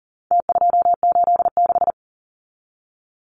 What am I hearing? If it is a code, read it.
Morse code, heard as a keyed tone, about 40 wpm, 700 Hz. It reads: T286
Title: Numbers - Contest Style - 4-Digit 40wpm